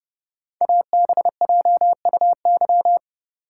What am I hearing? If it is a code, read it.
Morse code, heard as a keyed tone, about 31 wpm, 700 Hz. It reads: ABJUY
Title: Code Groups - 5 Letters - 30wpm